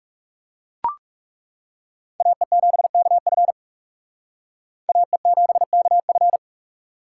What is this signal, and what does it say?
Morse code, heard as a keyed tone, about 37 wpm, 700 Hz. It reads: AE7KF AE7KF
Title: Speed Racing - USA 2x2 Call Signs 30wpm